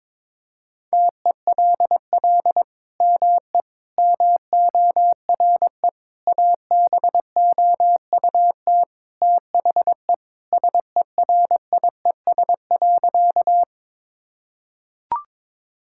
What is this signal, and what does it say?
Morse code, heard as a keyed tone, about 22 wpm, 700 Hz. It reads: TELL ME MORE ABOUT THE SERIES.
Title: Sentences from Top 1000 Words - Review Part 5 - 22wpm